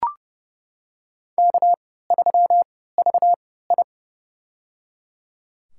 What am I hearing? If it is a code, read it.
Morse code, heard as a keyed tone, about 30 wpm, spaced as for 17 wpm, 700 Hz. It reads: K3VI
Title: USA Call Signs - 4 Characters - 3x Character Spacing - 30wpm